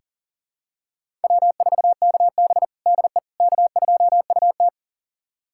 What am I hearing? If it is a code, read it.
Morse code, heard as a keyed tone, about 37 wpm, 700 Hz. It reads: W4KB DE K2UT
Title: USA Callsign DE Callsign - 4 Characters - 40wpm